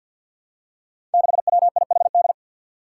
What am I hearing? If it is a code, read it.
Morse code, heard as a keyed tone, about 48 wpm, 700 Hz. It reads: 6WIHD
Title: Five Challenging Letter-Number Combinations 50wpm